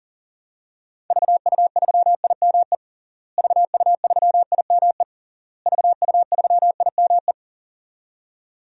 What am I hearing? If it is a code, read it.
Morse code, heard as a keyed tone, about 37 wpm, 700 Hz. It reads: VU3IME VU3IME VU3IME
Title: DXCC-100 Call Signs - 6 Characters - Repeated 3x - 40wpm